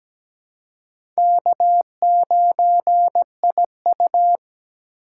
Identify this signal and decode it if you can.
Morse code, heard as a keyed tone, about 17 wpm, 700 Hz. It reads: K9IU
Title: USA Call Signs - 4 Characters - 4x Character Spacing - 17wpm